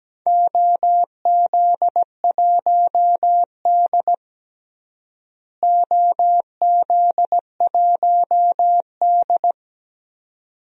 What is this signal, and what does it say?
Morse code, heard as a keyed tone, about 17 wpm, 700 Hz. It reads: OZ1D OZ1D
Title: DXCC-100 Call Signs - 4 Characters - Repeated 2x - 17wpm